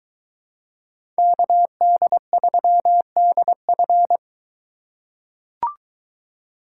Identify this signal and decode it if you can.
Morse code, heard as a keyed tone, about 22 wpm, 700 Hz. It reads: KD3DF